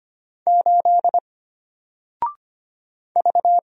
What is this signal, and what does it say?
Morse code, heard as a keyed tone, about 25 wpm, 700 Hz. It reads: 8 V
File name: Single-Letter-Number-25wpm